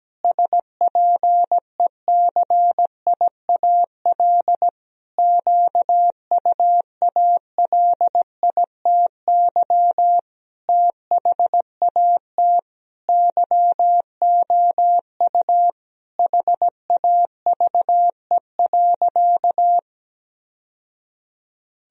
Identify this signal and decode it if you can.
Morse code, heard as a keyed tone, about 17 wpm, 700 Hz. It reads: SPECIAL QUALITY THAT YOU HAVE.
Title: Sentences from Top 1000 Words - Review Part 4 - 17wpm